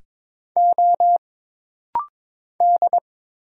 Morse code, heard as a keyed tone, about 22 wpm, 700 Hz. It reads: O D
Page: This rapid-fire timing is geared to push your ICR (Instant Character Recognition) skills.